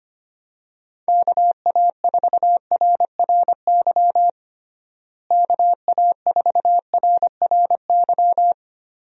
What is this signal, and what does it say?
Morse code, heard as a keyed tone, about 25 wpm, 700 Hz. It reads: KA4RRY KA4RRY
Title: USA Call Signs - 6 Characters - Repeated 2x - 25wpm